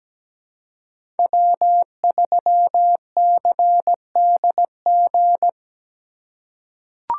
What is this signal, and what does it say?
Morse code, heard as a keyed tone, about 17 wpm, 700 Hz. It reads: W3CDG
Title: USA Call Signs - 5 Characters - 4x Character Spacing - 17wpm